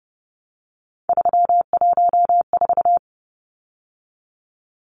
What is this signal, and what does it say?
Morse code, heard as a keyed tone, about 30 wpm, 700 Hz. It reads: 314
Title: Numbers - 3-Digit 30wpm